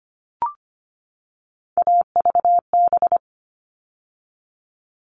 Morse code, heard as a keyed tone, about 25 wpm, 700 Hz. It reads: AVB
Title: All Cut Numbers - 3-Digit 25wpm